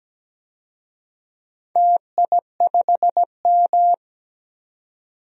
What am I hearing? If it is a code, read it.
Morse code, heard as a keyed tone, about 17 wpm, 700 Hz. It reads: TI5M
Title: DXCC-100 Call Signs - 4 Characters - 17wpm